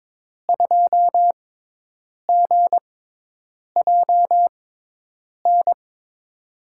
Morse code, heard as a keyed tone, about 22 wpm, spaced as for 8 wpm, 700 Hz. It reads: 2GJN
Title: USA Call Signs - 6 Characters - 6x Character Spacing - 22wpm